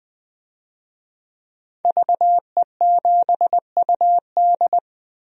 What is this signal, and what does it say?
Morse code, heard as a keyed tone, about 20 wpm, 700 Hz. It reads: VE7UD
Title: DXCC-100 Call Signs - 5 Characters - 20wpm